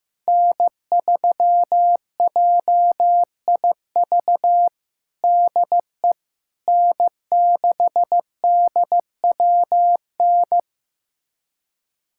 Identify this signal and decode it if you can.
Morse code, heard as a keyed tone, about 15 wpm, 700 Hz. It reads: N3JIV DE N6DWN
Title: USA Callsign DE Callsign - 5 Characters - 15wpm